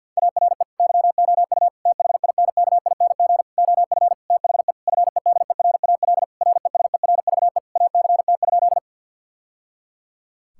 Morse code, heard as a keyed tone, about 50 wpm, 700 Hz. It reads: ARE YOU THINKING OF THE FEDERAL RESERVE ACT?
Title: Sentences from Top 1000 Words - Review Part 6 - 50wpm